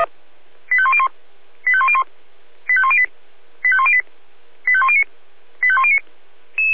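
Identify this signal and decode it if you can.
Five-tone selective call (ZVEI), heard as quick runs of five tones, tones of about 70 ms. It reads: 85101 85101 85108 85108 85109 85109
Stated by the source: ZVEI five-tone selective call received on SDR.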